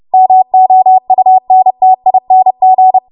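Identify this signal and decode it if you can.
Morse code, heard as a keyed tone, about 30 wpm, 750 Hz. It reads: MOUNTING
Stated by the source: Morse practice transmission at 30 WPM.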